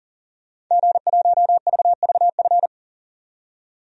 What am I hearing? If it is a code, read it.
Morse code, heard as a keyed tone, about 37 wpm, 700 Hz. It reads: G1VVF
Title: DXCC-100 Call Signs - 5 Characters - 40wpm